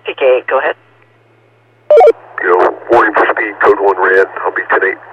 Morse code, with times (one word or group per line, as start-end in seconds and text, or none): none